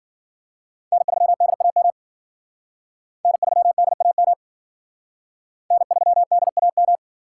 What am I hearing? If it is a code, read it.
Morse code, heard as a keyed tone, about 50 wpm, 700 Hz. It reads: N3DAM N3DAM N3DAM
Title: Speed Racing - USA 1x3 Call Signs 40wpm